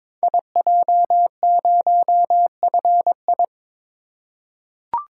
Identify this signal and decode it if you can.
Morse code, heard as a keyed tone, about 22 wpm, 700 Hz. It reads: IJ0FI
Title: Code Groups - 5 Letter:Numbers - 22wpm